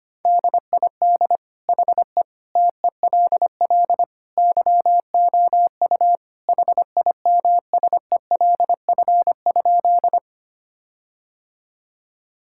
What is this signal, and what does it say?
Morse code, heard as a keyed tone, about 25 wpm, 700 Hz. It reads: DID HE TELL YOU HIMSELF?
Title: Sentences from Top 1600 Words 25wpm